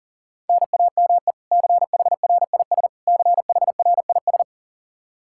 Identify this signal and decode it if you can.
Morse code, heard as a keyed tone, about 40 wpm, 700 Hz. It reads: NAME CHRIS CHRIS
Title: Practice QSO - Exchange 1 - Beginning - 40wpm